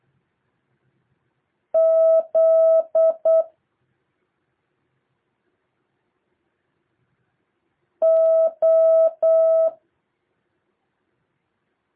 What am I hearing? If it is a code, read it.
Morse code, heard as a keyed tone, about 8 wpm, 650 Hz. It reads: Z O
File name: jh8cbh / 2023cw